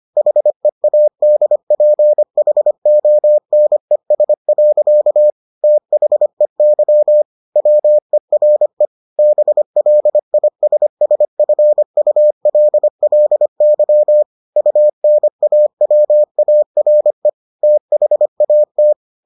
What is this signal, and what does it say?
Morse code, heard as a keyed tone, about 25 wpm, 600 Hz. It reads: HEADPHONES. THEY WERE BLISSFULLY UNAWARE THAT